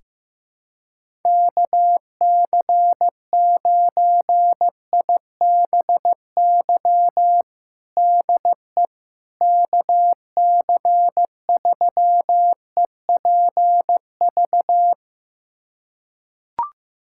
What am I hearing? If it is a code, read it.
Morse code, heard as a keyed tone, about 16 wpm, 700 Hz. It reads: KC9IBY DE KC3EPV
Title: USA Callsign DE Callsign - 6 Characters - 15wpm